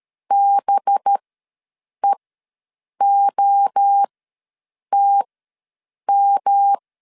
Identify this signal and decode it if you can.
Morse code, heard as a keyed tone, about 13 wpm, spaced as for 7 wpm, 800 Hz. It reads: BEOTM
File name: LEZIONI / 5 LBVC / 5 LBV ws 12wpm 3015